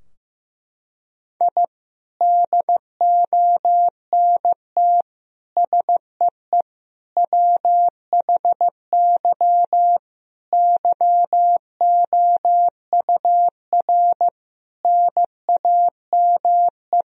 Morse code, heard as a keyed tone, about 16 wpm, 700 Hz. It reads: I DONT SEE WHY YOUR NAME
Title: Sentences from Top 1000 Words - Review Part 2 - 15wpm